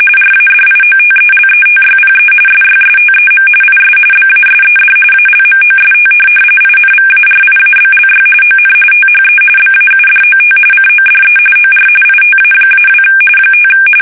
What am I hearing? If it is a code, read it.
RTTY, about 45.45 baud. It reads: RADIOTELETYPE SYSTEM IS BASED ALMOST INVARIABLY ON THE BAUDOT CODE OR ITA-2 5
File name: fsk850 / teletype-RTTY